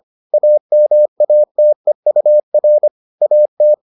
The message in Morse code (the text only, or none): AMATEUR AT